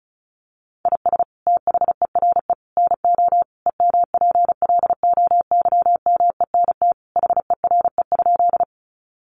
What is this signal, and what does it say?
Morse code, heard as a keyed tone, about 34 wpm, 700 Hz. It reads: IS THERE NO EMPLOYMENT HERE?
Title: Sentences from Top 1500 Words 35wpm